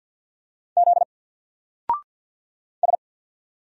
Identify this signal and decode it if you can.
Morse code, heard as a keyed tone, about 48 wpm, 700 Hz. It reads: Z S
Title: Single Character (Letter:Number:Punctuation:Prosign) Alt 50wpm